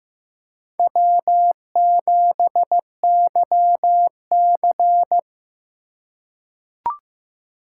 Morse code, heard as a keyed tone, about 15 wpm, 700 Hz. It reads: W7YC
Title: USA Call Signs - 4 Characters - 4x Character Spacing - 15wpm